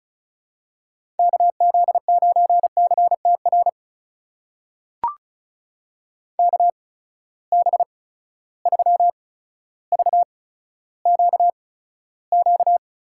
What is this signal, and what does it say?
Morse code, heard as a keyed tone, about 36 wpm, spaced as for 15 wpm, 700 Hz. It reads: KZ9CTR KB3VQQ
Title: USA Call Signs - 6 Characters - 8x Character Spacing - 35wpm